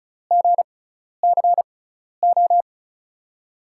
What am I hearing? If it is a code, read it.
Morse code, heard as a keyed tone, about 36 wpm, spaced as for 12 wpm, 700 Hz. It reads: GCO
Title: USA Call Signs - 6 Characters - 6x Character Spacing - 35wpm